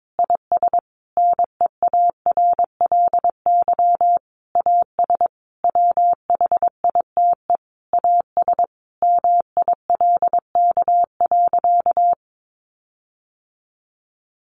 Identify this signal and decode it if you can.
Morse code, heard as a keyed tone, about 22 wpm, 700 Hz. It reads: IS NEARLY AS WHITE AS MILK.
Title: Sentences from Top 1500 Words 22wpm